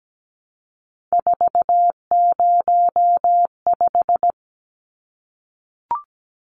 Morse code, heard as a keyed tone, about 17 wpm, 700 Hz. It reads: 405
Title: Numbers - 3-Digit 17wpm